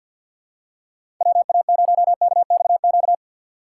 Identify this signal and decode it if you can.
Morse code, heard as a keyed tone, about 53 wpm, 700 Hz. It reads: WA0KXQ
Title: USA Call Signs - 6 Characters - 6x Character Spacing - 50wpm